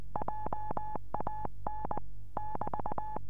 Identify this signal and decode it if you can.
Morse code, heard as a keyed tone, about 19 wpm, 850 Hz. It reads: JAN <BT>
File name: 432 eme / f6eti-dl9kr 432